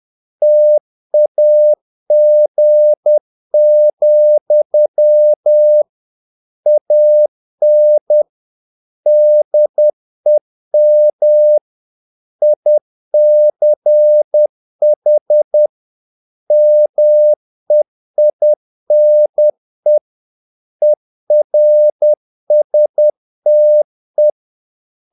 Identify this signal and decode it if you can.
Morse code, heard as a keyed tone, about 10 wpm, 600 Hz. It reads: TAG, AN DEM ICH MEINE ERSTE
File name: dk5ke / Humor-50BpM